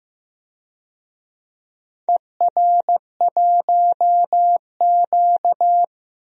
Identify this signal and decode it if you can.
Morse code, heard as a keyed tone, about 15 wpm, 700 Hz. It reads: ER1Q
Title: DXCC-100 Call Signs - 4 Characters - 15wpm